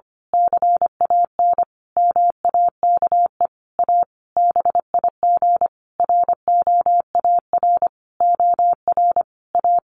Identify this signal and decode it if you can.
Morse code, heard as a keyed tone, about 25 wpm, 700 Hz. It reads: CAN MAKE A BIG ROAR OR A
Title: A Sound Of Thunder 25wpm